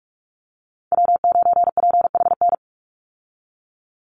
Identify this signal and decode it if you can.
Morse code, heard as a keyed tone, about 46 wpm, 700 Hz. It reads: W9PHN
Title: USA Call Signs 45wpm